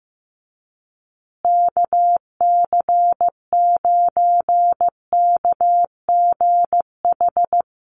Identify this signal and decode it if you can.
Morse code, heard as a keyed tone, about 15 wpm, 700 Hz. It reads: KC9KGH
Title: USA Call Signs Repeated 3x 15wpm